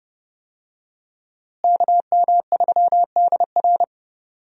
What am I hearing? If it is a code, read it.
Morse code, heard as a keyed tone, about 30 wpm, 700 Hz. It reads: KM3DR
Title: USA Call Signs - 5 Characters - Repeated 3x - 30wpm